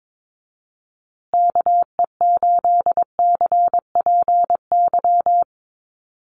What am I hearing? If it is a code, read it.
Morse code, heard as a keyed tone, about 22 wpm, 700 Hz. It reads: KE8CPY
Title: USA Call Signs Repeated 3x 22wpm